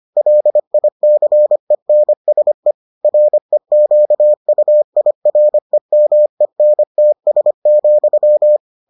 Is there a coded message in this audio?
Yes: Morse code from 0.2 to 8.6 s, heard as a keyed tone, about 25 wpm, 600 Hz. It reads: LICENSE REQUIREMENTS,